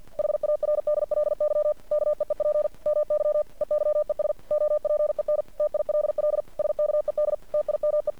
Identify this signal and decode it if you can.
Morse code, heard as a keyed tone, about 50 wpm, 600 Hz. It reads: HARDLY KEEP MY EYES OPEN TILL SKED TIME
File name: archived / CRSnet